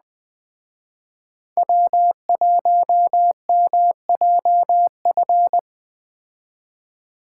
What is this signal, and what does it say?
Morse code, heard as a keyed tone, about 21 wpm, 700 Hz. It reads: W1MJF